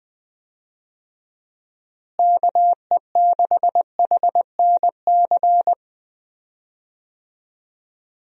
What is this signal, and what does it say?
Morse code, heard as a keyed tone, about 20 wpm, 700 Hz. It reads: KE6HNC
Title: USA Call Signs - 6 Characters - Repeated 2x - 20wpm